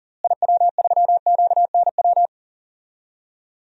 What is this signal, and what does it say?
Morse code, heard as a keyed tone, about 40 wpm, 700 Hz. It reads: IW3QNW
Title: DXCC-100 Call Signs - 6 Characters - Repeated 3x - 40wpm